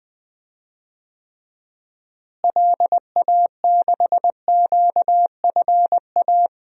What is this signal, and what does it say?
Morse code, heard as a keyed tone, about 20 wpm, 700 Hz. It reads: LA6QFA
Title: DXCC-100 Call Signs - 6 Characters - Repeated 2x - 20wpm